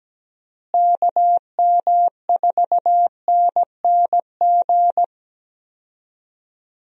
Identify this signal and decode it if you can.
Morse code, heard as a keyed tone, about 17 wpm, 700 Hz. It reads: KM4NNG
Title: USA Call Signs - 6 Characters - Repeated 2x - 17wpm